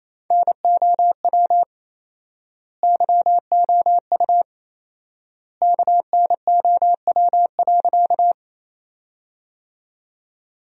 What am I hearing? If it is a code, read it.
Morse code, heard as a keyed tone, about 28 wpm, 700 Hz. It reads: NOW YOU KNOW.
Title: 4x Word-Spacing - Sentences from Top 100 Words - 28wpm